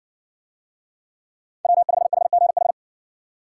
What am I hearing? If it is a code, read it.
Morse code, heard as a keyed tone, about 66 wpm, 700 Hz. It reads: W5HGH